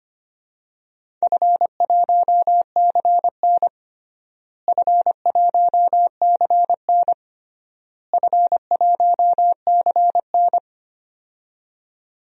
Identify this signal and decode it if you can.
Morse code, heard as a keyed tone, about 25 wpm, 700 Hz. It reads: F1CN F1CN F1CN
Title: DXCC-100 Call Signs - 4 Characters - Repeated 3x - 25wpm